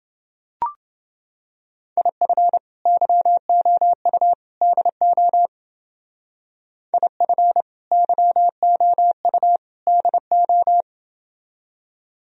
Morse code, heard as a keyed tone, about 28 wpm, 700 Hz. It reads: IF YOU DO IF YOU DO